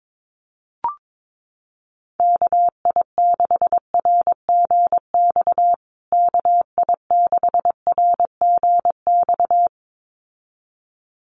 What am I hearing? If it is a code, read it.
Morse code, heard as a keyed tone, about 22 wpm, 700 Hz. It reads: KI6RGX KI6RGX
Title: USA Call Signs Repeated 2x 22wpm